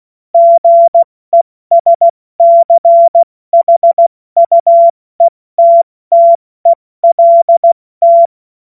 Morse code, heard as a keyed tone, about 16 wpm, spaced as for 14 wpm, 680 Hz. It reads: GESCHUETTELT
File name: Morsekurs / Witze / Witz 23. Dezember 2025 70 F80